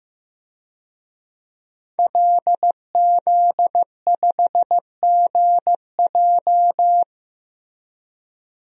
Morse code, heard as a keyed tone, about 15 wpm, 700 Hz. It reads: LZ5GJ
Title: DXCC-100 Call Signs - 5 Characters - 15wpm